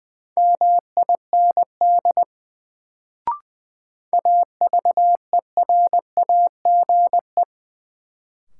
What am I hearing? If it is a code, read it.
Morse code, heard as a keyed tone, about 20 wpm, 700 Hz. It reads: MIND AVERAGE
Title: Sets of 5 Words - New Word Drill - 20wpm